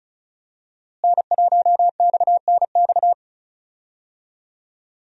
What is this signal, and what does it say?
Morse code, heard as a keyed tone, about 36 wpm, 700 Hz. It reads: N1XNX
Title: USA Call Signs - 5 Characters - Repeated 3x - 35wpm